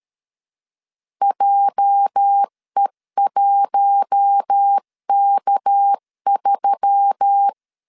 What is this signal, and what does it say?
Morse code, heard as a keyed tone, about 13 wpm, 800 Hz. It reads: JE1K3
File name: LEZIONI / 9 286 12wpm 2305